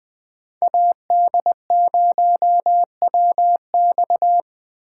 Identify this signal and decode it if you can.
Morse code, heard as a keyed tone, about 21 wpm, 700 Hz. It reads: AD0WX